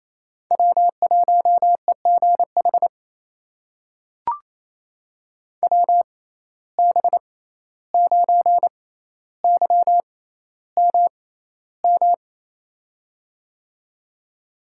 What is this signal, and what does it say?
Morse code, heard as a keyed tone, about 27 wpm, spaced as for 14 wpm, 700 Hz. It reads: W1EGH WB9YMM
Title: USA Call Signs - 6x Character Spacing - 28wpm